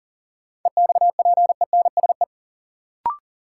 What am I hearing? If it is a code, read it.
Morse code, heard as a keyed tone, about 40 wpm, 700 Hz. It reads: EXPENSE
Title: Sets of 4 Words - New Word Drill - 40wpm